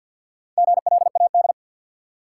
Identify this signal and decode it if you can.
Morse code, heard as a keyed tone, about 53 wpm, 700 Hz. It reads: GLAD
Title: Sentences from Top 800 Words - New Word Drill - 50wpm